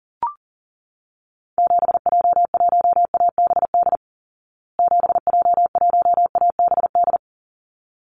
Morse code, heard as a keyed tone, about 40 wpm, 700 Hz. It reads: 7J1ABD 7J1ABD
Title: Call Signs Repeated 3x - 40wpm